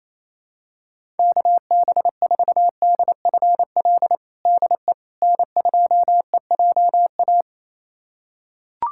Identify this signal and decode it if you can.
Morse code, heard as a keyed tone, about 28 wpm, 700 Hz. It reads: KB4DFL DE N2EJA
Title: USA Callsign DE Callsign 28wpm